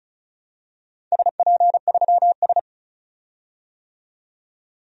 Morse code, heard as a keyed tone, about 36 wpm, 700 Hz. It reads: SP3S
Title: DXCC-100 Call Signs - 4 Characters - 35wpm